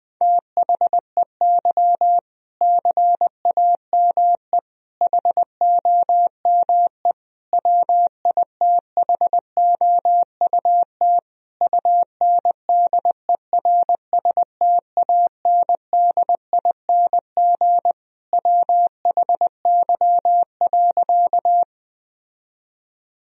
Morse code, heard as a keyed tone, about 21 wpm, 700 Hz. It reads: THEY CAME HOME WITHOUT UNDERSTANDING WHY.
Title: Sentences from Top 1000 Words - Review Part 3 - 20wpm